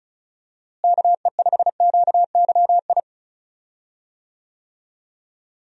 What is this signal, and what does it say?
Morse code, heard as a keyed tone, about 34 wpm, 700 Hz. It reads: KE5QYI
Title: USA Call Signs - 6 Characters - Repeated 2x - 35wpm